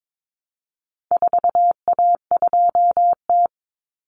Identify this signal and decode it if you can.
Morse code, heard as a keyed tone, about 22 wpm, 700 Hz. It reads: 4A2T
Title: Common Cut Numbers - 4-Digit 22wpm